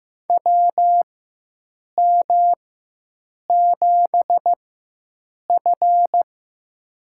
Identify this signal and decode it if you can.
Morse code, heard as a keyed tone, about 15 wpm, spaced as for 7 wpm, 700 Hz. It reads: WM7F
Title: USA Call Signs - 4 Characters - 4x Character Spacing - 15wpm